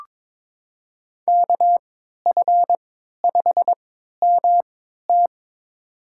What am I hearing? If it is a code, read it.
Morse code, heard as a keyed tone, about 22 wpm, spaced as for 13 wpm, 700 Hz. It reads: KF5MT
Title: USA Call Signs - 5 Characters - 3x Character Spacing - 22wpm